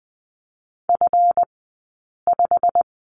Morse code, heard as a keyed tone, about 20 wpm, spaced as for 8 wpm, 700 Hz. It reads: F5
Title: 165 Lesson 27d - Number 7 - Two Characters 20wpm